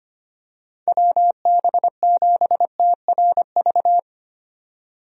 Morse code, heard as a keyed tone, about 25 wpm, 700 Hz. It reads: WB7TRV
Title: USA Call Signs - 6 Characters - 25wpm